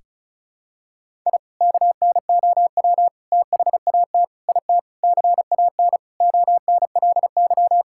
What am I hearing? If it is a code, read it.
Morse code, heard as a keyed tone, about 36 wpm, 700 Hz. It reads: I KNOW THAT IT CAN ONLY
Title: Sentences from Top 1000 Words - Review Part 6 - 35wpm